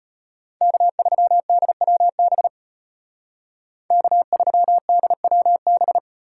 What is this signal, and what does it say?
Morse code, heard as a keyed tone, about 33 wpm, 700 Hz. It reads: K3DWB K3DWB